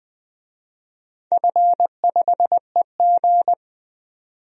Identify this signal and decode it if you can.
Morse code, heard as a keyed tone, about 20 wpm, 700 Hz. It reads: F5EG
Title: DXCC-100 Call Signs - 4 Characters - 20wpm